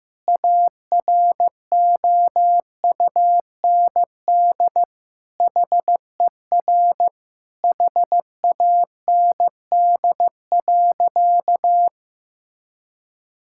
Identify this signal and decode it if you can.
Morse code, heard as a keyed tone, about 15 wpm, 700 Hz. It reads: AROUND HER HAND.
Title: Sentences from Top 1600 Words 15wpm